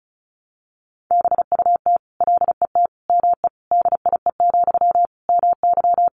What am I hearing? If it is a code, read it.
Morse code, heard as a keyed tone, about 34 wpm, 700 Hz. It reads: BUT LET ME DIE, MY
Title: Sentences from Top 1300 Words 35wpm